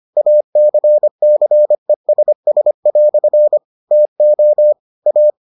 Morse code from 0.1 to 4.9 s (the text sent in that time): ACCESS" TO